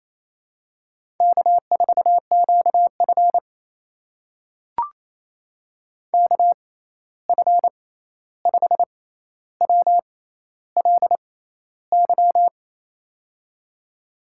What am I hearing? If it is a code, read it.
Morse code, heard as a keyed tone, about 27 wpm, spaced as for 13 wpm, 700 Hz. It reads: K4QF KF5WLY
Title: USA Call Signs - 6x Character Spacing - 28wpm